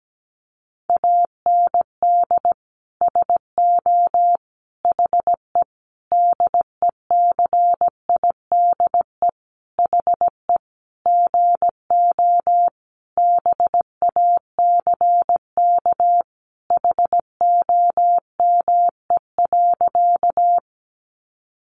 Morse code, heard as a keyed tone, about 17 wpm, 700 Hz. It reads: AND SO HE DECIDE HE GO BACK HOME.
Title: Sentences from Top 500 Words 17wpm